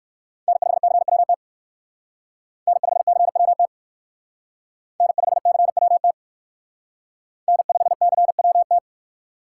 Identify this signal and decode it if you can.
Morse code, heard as a keyed tone, about 54 wpm, 700 Hz. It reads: N5CWT N5CWT N5CWT N5CWT
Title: Speed Racing - USA 1x3 Call Signs 45wpm